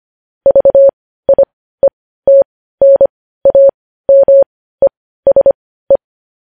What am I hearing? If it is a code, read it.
Morse code, heard as a keyed tone, about 25 wpm, spaced as for 15 wpm, 550 Hz. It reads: VIETNAMESE